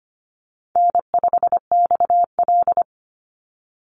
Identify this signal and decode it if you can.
Morse code, heard as a keyed tone, about 25 wpm, 700 Hz. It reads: N5XL
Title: NA SOTA Chaser Callsigns 25wpm